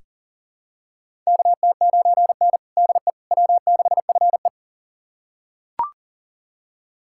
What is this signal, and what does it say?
Morse code, heard as a keyed tone, about 40 wpm, 700 Hz. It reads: KT9N DE W6FE
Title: USA Callsign DE Callsign - 4 Characters - 40wpm